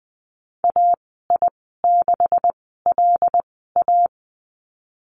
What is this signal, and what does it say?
Morse code, heard as a keyed tone, about 20 wpm, spaced as for 15 wpm, 700 Hz. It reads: AI6LA
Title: 167 Lesson 27g - Number 7 - Call Signs 20wpm